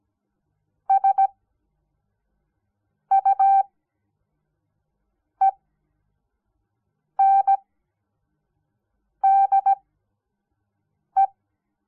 Morse code, heard as a keyed tone, about 17 wpm, spaced as for 5 wpm, 780 Hz. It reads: SUENDE
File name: Onlinelehrgang / morsen / DJ4UF Morsen06